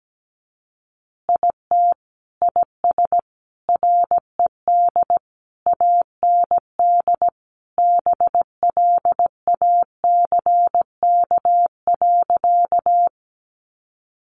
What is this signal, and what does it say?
Morse code, heard as a keyed tone, about 17 wpm, 700 Hz. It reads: IT IS RED AND BLACK.
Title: Additional Easy Phrases 17wpm